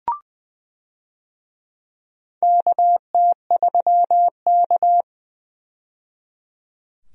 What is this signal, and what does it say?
Morse code, heard as a keyed tone, about 20 wpm, 700 Hz. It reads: KT3K
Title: USA Call Signs - 4 Characters - 20wpm